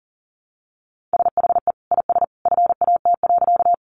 Morse code, heard as a keyed tone, about 40 wpm, 700 Hz. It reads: SHE IS FAT.
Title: Sentences from Top 600 Words 40wpm